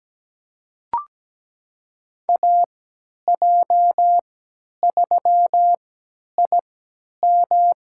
Morse code, heard as a keyed tone, about 17 wpm, spaced as for 10 wpm, 700 Hz. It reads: AJ3IM
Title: USA Call Signs - 5 Characters - 3x Character Spacing - 17wpm